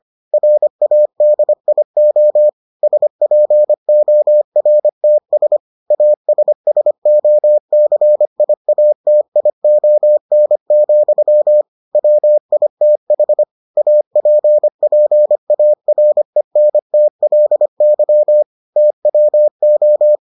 Morse code, heard as a keyed tone, about 25 wpm, 600 Hz. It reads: RADIO SPORTS ASSOCIATION, WITH APPARENTLY TWO